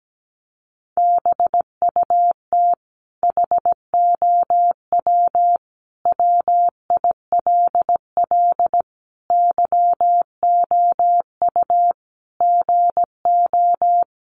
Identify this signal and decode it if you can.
Morse code, heard as a keyed tone, about 17 wpm, 700 Hz. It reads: BUT HOW WILL YOU GO
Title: Sentences from Top 1100 Words 17wpm